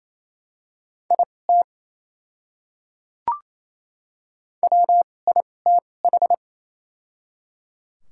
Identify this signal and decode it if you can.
Morse code, heard as a keyed tone, about 28 wpm, spaced as for 20 wpm, 700 Hz. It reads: IT WITH
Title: 2x Character Spacing - Top 200 Words - 28wpm